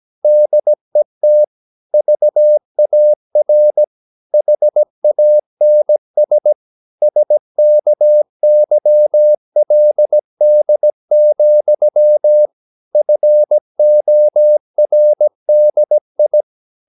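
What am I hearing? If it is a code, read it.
Morse code, heard as a keyed tone, about 17 wpm, 600 Hz. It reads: DET VAR HANS SKYLD, FORDI
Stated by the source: Slow (QRS) Morse practice at 17 wpm, from CW med Gnister.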